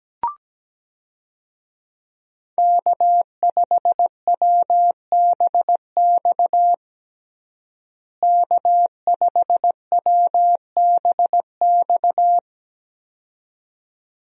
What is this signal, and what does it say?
Morse code, heard as a keyed tone, about 17 wpm, 700 Hz. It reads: K5WBX K5WBX
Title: USA Call Signs - 5 Characters - Repeated 2x - 17wpm